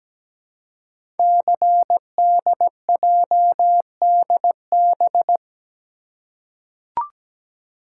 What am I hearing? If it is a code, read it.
Morse code, heard as a keyed tone, about 17 wpm, 700 Hz. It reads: CDJDB
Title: Code Groups - 5 Letters - 17wpm